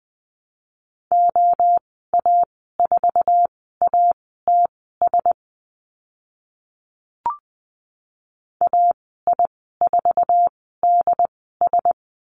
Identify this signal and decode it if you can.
Morse code, heard as a keyed tone, about 20 wpm, spaced as for 15 wpm, 700 Hz. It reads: OA4ATS AI4DS
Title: 082 Lesson 14g - Number 2 - Call Signs 20wpm